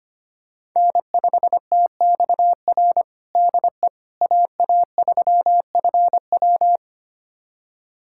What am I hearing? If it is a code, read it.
Morse code, heard as a keyed tone, about 25 wpm, 700 Hz. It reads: N5TXR DE AA3FW
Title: USA Callsign DE Callsign - 5 Characters - 25wpm